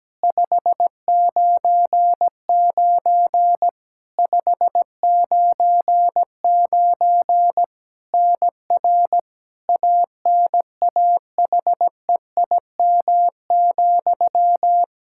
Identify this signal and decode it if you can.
Morse code, heard as a keyed tone, about 17 wpm, 700 Hz. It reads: 599 599 NR ANAHEIM,
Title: Practice QSO - Exchange 1 - Beginning - 17wpm